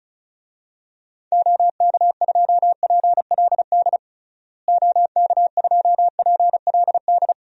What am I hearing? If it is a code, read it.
Morse code, heard as a keyed tone, about 36 wpm, 700 Hz. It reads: OK2PLD OK2PLD
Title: DXCC-100 Call Signs - 6 Characters - Repeated 3x - 35wpm